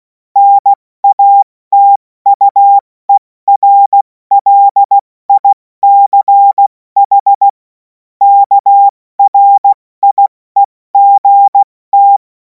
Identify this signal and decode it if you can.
Morse code, heard as a keyed tone, about 16 wpm, 800 Hz. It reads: NATUERLICH KRIEGT
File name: Morsekurs / Witze / Witz 17. März 2026 70 F80